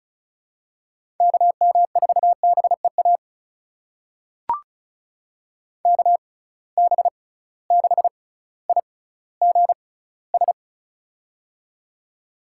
Morse code, heard as a keyed tone, about 36 wpm, spaced as for 18 wpm, 700 Hz. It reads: KM4BEA KB6IGS
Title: USA Call Signs - 6 Characters - 6x Character Spacing - 35wpm